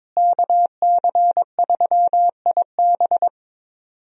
Morse code, heard as a keyed tone, about 22 wpm, 700 Hz. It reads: KC3IB
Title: USA Call Signs - 5 Characters - Repeated 3x - 22wpm